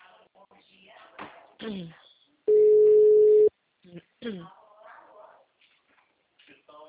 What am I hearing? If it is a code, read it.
Ringback tone (Europe, 425 Hz), with speech.
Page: TLML PRUDENTE